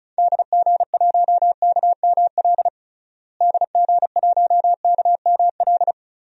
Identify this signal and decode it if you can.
Morse code, heard as a keyed tone, about 36 wpm, 700 Hz. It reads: DG1KML DG1KML
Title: DXCC-100 Call Signs - 6 Characters - Repeated 3x - 35wpm